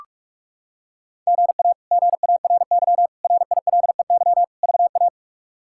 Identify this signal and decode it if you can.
Morse code, heard as a keyed tone, about 45 wpm, 700 Hz. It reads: GA GARY RILEY VA
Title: K1USN SST QSO Practice - 45wpm